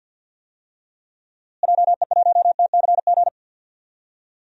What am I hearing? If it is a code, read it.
Morse code, heard as a keyed tone, about 51 wpm, 700 Hz. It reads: JE1TCG
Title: DXCC-100 Call Signs - 6 Characters - Repeated 3x - 50wpm